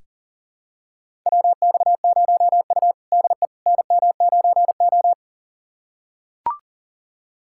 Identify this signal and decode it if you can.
Morse code, heard as a keyed tone, about 38 wpm, 700 Hz. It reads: WX0U DE NM9O
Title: USA Callsign DE Callsign - 4 Characters - 40wpm